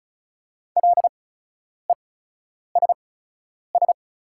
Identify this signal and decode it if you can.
Morse code, heard as a keyed tone, about 34 wpm, spaced as for 10 wpm, 690 Hz. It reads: LESS
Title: 8x Character Spacing - Top 500 Words - 35wpm